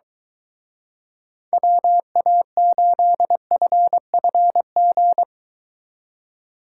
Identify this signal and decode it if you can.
Morse code, heard as a keyed tone, about 23 wpm, 700 Hz. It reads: WA8FFG